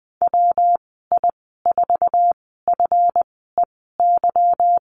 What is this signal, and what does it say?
Morse code, heard as a keyed tone, about 20 wpm, 700 Hz. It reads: W I 4 F E Y
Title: 231 Lesson 37g - Letter x - Call Signs 20wpm